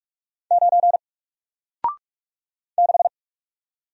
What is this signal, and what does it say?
Morse code, heard as a keyed tone, about 47 wpm, 700 Hz. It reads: 9 6
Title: Single Letter-Number Alt 45wpm